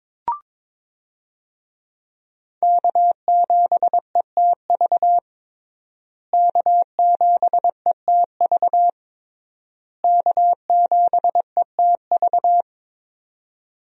Morse code, heard as a keyed tone, about 22 wpm, 700 Hz. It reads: K7ETV K7ETV K7ETV
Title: USA Call Signs - 5 Characters - Repeated 3x - 22wpm